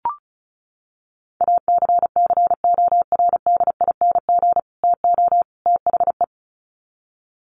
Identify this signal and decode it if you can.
Morse code, heard as a keyed tone, about 34 wpm, 700 Hz. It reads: ACCORDING TO THE
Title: Sets of 3 Words 35wpm